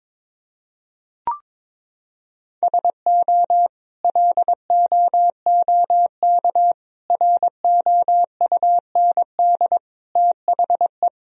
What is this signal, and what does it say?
Morse code, heard as a keyed tone, about 21 wpm, 700 Hz. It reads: SO LOOK ROUND THE
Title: Sentences from Top 1000 Words - Review Part 1 - 22wpm